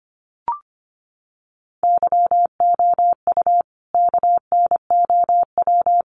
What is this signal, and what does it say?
Morse code, heard as a keyed tone, about 25 wpm, 700 Hz. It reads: YOU KNOW
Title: Sets of 3 Words 25wpm